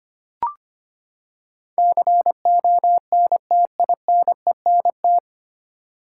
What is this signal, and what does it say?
Morse code, heard as a keyed tone, about 25 wpm, 700 Hz. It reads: CONTINENT
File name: Nine Letter Words - 25wpm